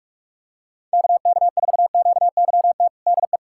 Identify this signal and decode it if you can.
Morse code, heard as a keyed tone, about 41 wpm, 700 Hz. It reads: KK4QYT DE
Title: USA Callsign DE Callsign - 6 Characters - 45wpm